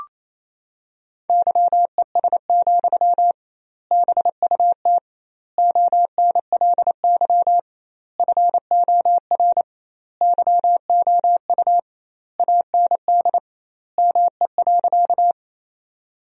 Morse code, heard as a keyed tone, about 28 wpm, 700 Hz. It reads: YES, BUT ONLY FOR YOU AND ME.
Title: 2x Word-Spacing - Sentences from Top 500 Words - 28wpm